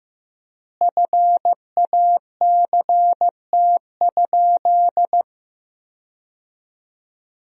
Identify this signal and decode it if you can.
Morse code, heard as a keyed tone, about 15 wpm, 700 Hz. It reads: FACT?
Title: 3x Word-Spacing - Sentences from Top 500 Words - 15wpm